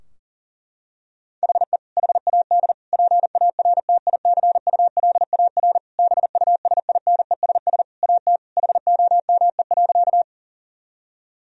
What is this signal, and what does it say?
Morse code, heard as a keyed tone, about 37 wpm, 700 Hz. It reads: HE HAD PARTICULAR BUSINESS AT HOME.
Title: Sentences from Top 1000 Words - Review Part 4 - 40wpm